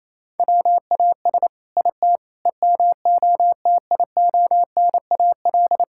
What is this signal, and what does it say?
Morse code, heard as a keyed tone, about 28 wpm, 700 Hz. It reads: WAS IT EMOTIONAL
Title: Sentences from Top 1000 Words - Review Part 5 - 28wpm